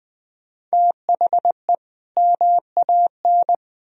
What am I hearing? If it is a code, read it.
Morse code, heard as a keyed tone, about 20 wpm, 700 Hz. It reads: THE MAN
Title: Speed-Racing - Sets of 2 Words 17wpm